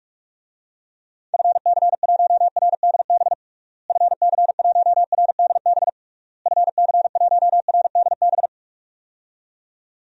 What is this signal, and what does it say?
Morse code, heard as a keyed tone, about 42 wpm, 700 Hz. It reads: FC1RDB FC1RDB FC1RDB
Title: DXCC-100 Call Signs - 6 Characters - Repeated 3x - 45wpm